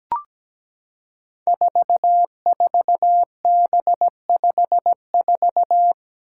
Morse code, heard as a keyed tone, about 17 wpm, 700 Hz. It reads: 44B54
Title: Five Challenging Letter-Number Combinations 17wpm